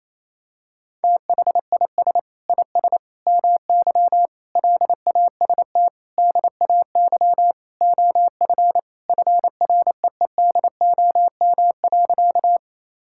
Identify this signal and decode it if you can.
Morse code, heard as a keyed tone, about 28 wpm, 700 Hz. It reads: THIS IS MY LAST DAY OF FREEDOM.
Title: Sentences from Top 1000 Words - Review Part 5 - 28wpm